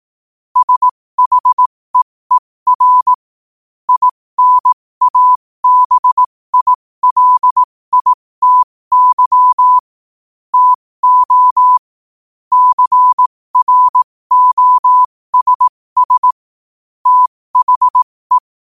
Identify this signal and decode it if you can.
Morse code, heard as a keyed tone, about 18 wpm, spaced as for 15 wpm, 1000 Hz. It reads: SHEER INABILITY TO CROSS THE
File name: qotdfiles / 15wpm